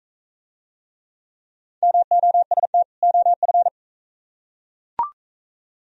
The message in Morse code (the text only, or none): MOST OF